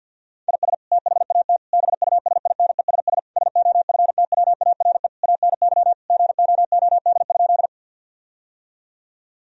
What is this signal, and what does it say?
Morse code, heard as a keyed tone, about 52 wpm, 700 Hz. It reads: IS THAT BUSINESS SOFTWARE ANY GOOD?
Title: Sentences from Top 1000 Words - Review Part 3 - 50wpm